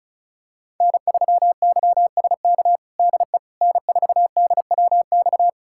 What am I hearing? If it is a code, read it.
Morse code, heard as a keyed tone, about 36 wpm, 690 Hz. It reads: N3YSK DE N4DWX
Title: USA Callsign DE Callsign - 5 Characters - 35wpm